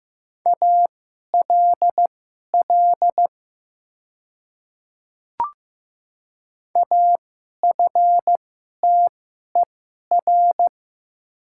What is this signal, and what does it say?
Morse code, heard as a keyed tone, about 15 wpm, spaced as for 11 wpm, 700 Hz. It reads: ALL AFTER
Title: 2x Character Spacing - Top 200 Words - 15wpm